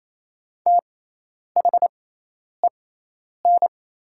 Morse code, heard as a keyed tone, about 27 wpm, spaced as for 10 wpm, 700 Hz. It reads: THEN
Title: 6x Character Spacing - Top 200 Words - 28wpm